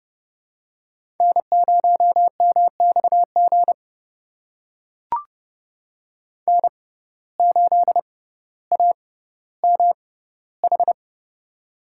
Morse code, heard as a keyed tone, about 30 wpm, spaced as for 15 wpm, 700 Hz. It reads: N0MXG N8AMH
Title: USA Call Signs - 5 Characters - 6x Character Spacing - 30wpm